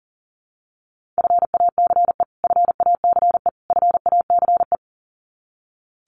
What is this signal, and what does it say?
Morse code, heard as a keyed tone, about 40 wpm, 700 Hz. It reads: FACE FACE FACE
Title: Top 500 Words Repeated 40wpm